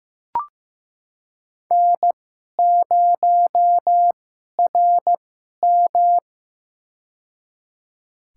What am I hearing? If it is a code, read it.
Morse code, heard as a keyed tone, about 15 wpm, spaced as for 11 wpm, 700 Hz. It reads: N0RM
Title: USA Call Signs - 4 Characters - 2x Character Spacing - 15wpm